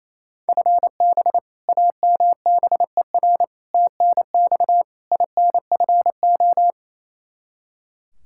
Morse code, heard as a keyed tone, about 28 wpm, 700 Hz. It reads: FB AMBER TNX INFO
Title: Practice QSO - Exchange 3 - More Info - 28wpm